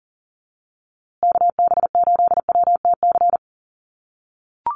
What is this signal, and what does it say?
Morse code, heard as a keyed tone, about 40 wpm, 700 Hz. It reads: KB8WTC
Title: USA Call Signs Repeated 2x 40wpm